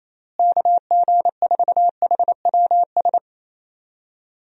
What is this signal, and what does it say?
Morse code, heard as a keyed tone, about 27 wpm, 700 Hz. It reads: KG4HWS
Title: USA Call Signs - 6 Characters - Repeated 2x - 28wpm